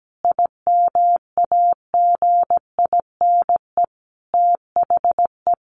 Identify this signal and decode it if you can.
Morse code, heard as a keyed tone, about 17 wpm, 700 Hz. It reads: IMAGINE THE
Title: Sentences from Top 1400 Words 17wpm